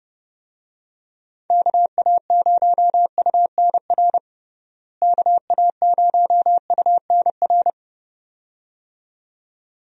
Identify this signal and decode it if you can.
Morse code, heard as a keyed tone, about 32 wpm, 700 Hz. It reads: KA0UNR KA0UNR
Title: USA Call Signs - 6 Characters - Repeated 2x - 30wpm